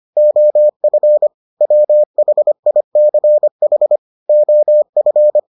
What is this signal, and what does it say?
Morse code, heard as a keyed tone, about 25 wpm, 600 Hz. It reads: OF WHICH OF